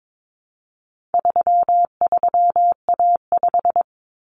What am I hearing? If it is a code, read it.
Morse code, heard as a keyed tone, about 22 wpm, 700 Hz. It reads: 33A5
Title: Common Cut Numbers - 4-Digit 22wpm